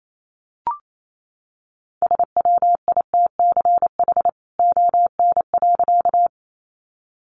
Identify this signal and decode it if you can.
Morse code, heard as a keyed tone, about 28 wpm, 700 Hz. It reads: SWITCH ON.
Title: A Sound Of Thunder 28wpm